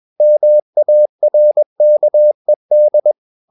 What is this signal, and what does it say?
Morse code, heard as a keyed tone, about 21 wpm, 600 Hz. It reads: MARKED
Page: CW med Gnister